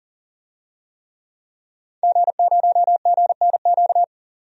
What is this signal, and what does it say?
Morse code, heard as a keyed tone, about 40 wpm, 700 Hz. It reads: G0GNQ
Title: DXCC-100 Call Signs - 5 Characters - 40wpm